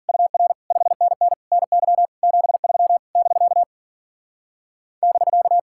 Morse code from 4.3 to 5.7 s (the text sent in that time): <BK>